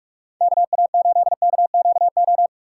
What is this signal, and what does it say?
Morse code, heard as a keyed tone, about 46 wpm, 700 Hz. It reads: KA8KQO
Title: USA Call Signs - 6 Characters - 2x Character Spacing - 45wpm